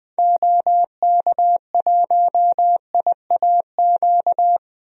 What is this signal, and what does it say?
Morse code, heard as a keyed tone, about 20 wpm, 700 Hz. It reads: OK1IAQ
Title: DXCC-100 Call Signs - 6 Characters - 20wpm